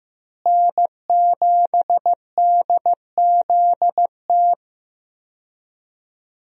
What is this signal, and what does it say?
Morse code, heard as a keyed tone, about 15 wpm, 700 Hz. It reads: N7DZT
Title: USA Call Signs - 5 Characters - Repeated 2x - 15wpm